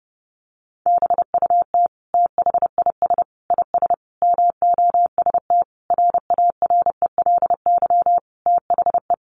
Morse code, heard as a keyed tone, about 30 wpm, 700 Hz. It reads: BUT THIS IS MOST RARELY THE
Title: Sentences from Top 1400 Words 30wpm